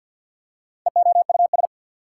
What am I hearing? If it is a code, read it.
Morse code, heard as a keyed tone, about 48 wpm, 690 Hz. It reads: EOUS
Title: Prefixes and Suffixes 50wpm